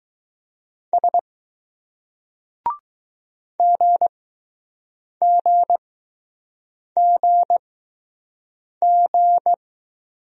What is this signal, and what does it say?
Morse code, heard as a keyed tone, about 20 wpm, spaced as for 6 wpm, 700 Hz. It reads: S GGGG